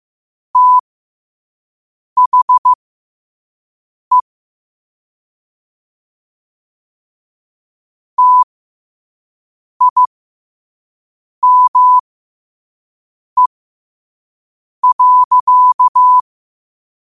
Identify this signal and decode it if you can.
Morse code, heard as a keyed tone, about 15 wpm, spaced as for 5 wpm, 1000 Hz. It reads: THE TIME.